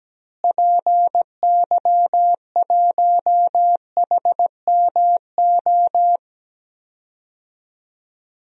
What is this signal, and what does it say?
Morse code, heard as a keyed tone, about 17 wpm, 700 Hz. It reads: PY1HMO
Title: DXCC-100 Call Signs - 6 Characters - Repeated 2x - 17wpm